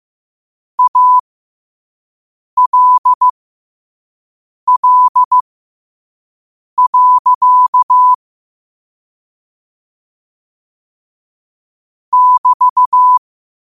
Morse code, heard as a keyed tone, about 15 wpm, spaced as for 5 wpm, 1000 Hz. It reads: ALL. <BT>
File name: qotdfiles / 5wpm